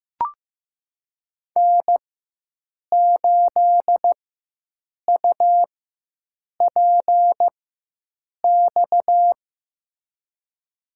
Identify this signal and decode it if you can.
Morse code, heard as a keyed tone, about 16 wpm, spaced as for 7 wpm, 700 Hz. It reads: N8UPX
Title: USA Call Signs - 4x Character Spacing - 15wpm